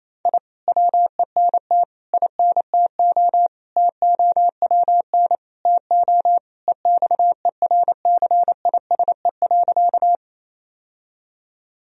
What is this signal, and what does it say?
Morse code, heard as a keyed tone, about 28 wpm, 700 Hz. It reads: I WENT INTO TOWN TO EXERCISE.
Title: Sentences from Top 1000 Words - Review Part 4 - 28wpm